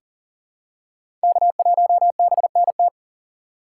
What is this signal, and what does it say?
Morse code, heard as a keyed tone, about 40 wpm, 700 Hz. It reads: K1BNT
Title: USA Call Signs - 5 Characters - 3x Character Spacing - 40wpm